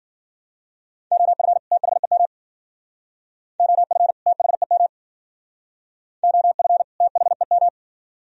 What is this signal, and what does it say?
Morse code, heard as a keyed tone, about 53 wpm, 700 Hz. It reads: OF THEM OF THEM OF THEM